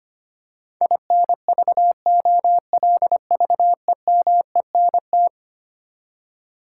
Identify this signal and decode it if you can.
Morse code, heard as a keyed tone, about 25 wpm, 700 Hz. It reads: INVOLVEMENT
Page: morse-code-ninja-practice-25wpm